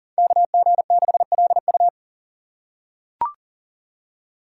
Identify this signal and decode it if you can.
Morse code, heard as a keyed tone, about 39 wpm, 690 Hz. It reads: KG6LU
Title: USA Call Signs - 5 Characters - 6x Character Spacing - 40wpm